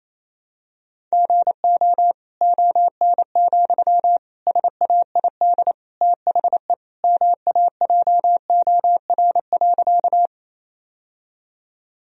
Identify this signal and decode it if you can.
Morse code, heard as a keyed tone, about 28 wpm, 700 Hz. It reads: GO ON, SAID THE MAJOR.
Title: Sentences from Top 1000 Words - Review Part 4 - 28wpm